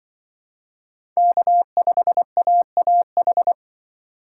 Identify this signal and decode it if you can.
Morse code, heard as a keyed tone, about 24 wpm, 700 Hz. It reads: K5AAH
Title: Speed Racing - USA 1x3 Call Signs 20wpm